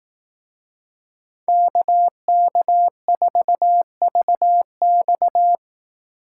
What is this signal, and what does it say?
Morse code, heard as a keyed tone, about 18 wpm, 700 Hz. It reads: KK4VX
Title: Speed Racing - USA 2x2 Call Signs 15wpm